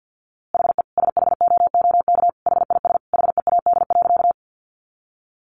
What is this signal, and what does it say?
Morse code, heard as a keyed tone, about 50 wpm, 700 Hz. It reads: HE SHOOK HIS HEAD.
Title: A Sound Of Thunder 50wpm